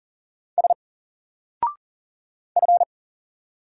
Morse code, heard as a keyed tone, about 40 wpm, 700 Hz. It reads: S F
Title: Single Character (Letter:Number:Punctuation:Prosign) 40wpm